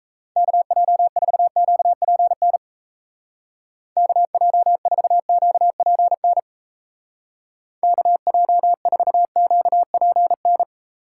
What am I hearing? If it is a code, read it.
Morse code, heard as a keyed tone, about 34 wpm, 700 Hz. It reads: KJ4QPN KJ4QPN KJ4QPN
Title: Speed Racing - USA 2x3 Call Signs 28wpm